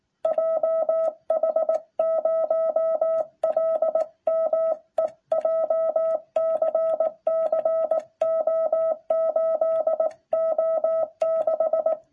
Morse code, heard as a keyed tone, about 17 wpm, 660 Hz. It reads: JH0LME JCC O8O6